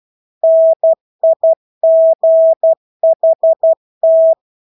Morse code, heard as a keyed tone, about 12 wpm, 650 Hz. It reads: NIGHT
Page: Engelske ord 12 wpm